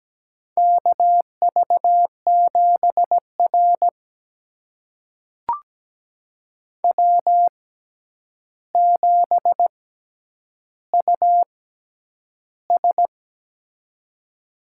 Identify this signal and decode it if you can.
Morse code, heard as a keyed tone, about 17 wpm, spaced as for 9 wpm, 700 Hz. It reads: KV7R W7US
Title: USA Call Signs - 4 Characters - 6x Character Spacing - 17wpm